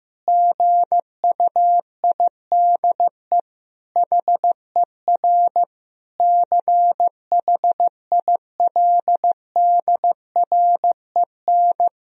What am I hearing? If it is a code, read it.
Morse code, heard as a keyed tone, about 15 wpm, 700 Hz. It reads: GUIDE HER CHILDREN
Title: Sentences from Top 1000 Words - Review Part 3 - 15wpm